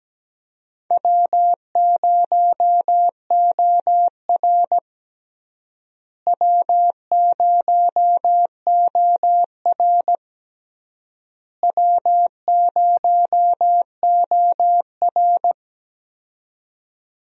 Morse code, heard as a keyed tone, about 17 wpm, 700 Hz. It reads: W0OR W0OR W0OR
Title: USA Call Signs - 4 Characters - Repeated 3x - 17wpm